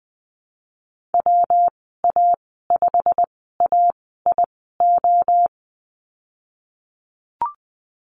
Morse code, heard as a keyed tone, about 20 wpm, spaced as for 15 wpm, 700 Hz. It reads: WA5AIO
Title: 147 Lesson 24g - Letter y - Call Signs 20wpm